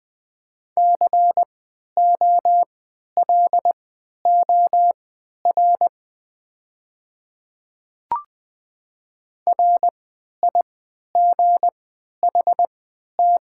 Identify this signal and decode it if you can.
Morse code, heard as a keyed tone, about 20 wpm, spaced as for 11 wpm, 700 Hz. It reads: COLOR RIGHT
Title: 3x Character Spacing - Top 300 Words - 20wpm